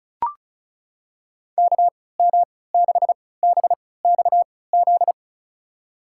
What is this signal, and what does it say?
Morse code, heard as a keyed tone, about 36 wpm, spaced as for 20 wpm, 700 Hz. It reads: KM6BXZ
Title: USA Call Signs - 6 Characters - 3x Character Spacing - 35wpm